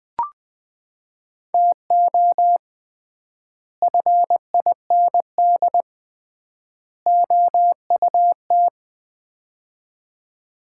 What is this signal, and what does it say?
Morse code, heard as a keyed tone, about 20 wpm, 700 Hz. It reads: TO FIND OUT
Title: 3x Word-Spacing - Sets of 3 Words - 20wpm